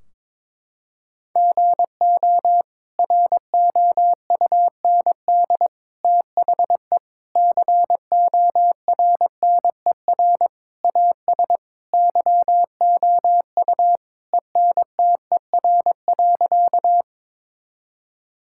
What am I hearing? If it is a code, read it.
Morse code, heard as a keyed tone, about 22 wpm, 700 Hz. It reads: GO ROUND THE CORNER AS YOU ENTER.
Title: Sentences from Top 1600 Words 22wpm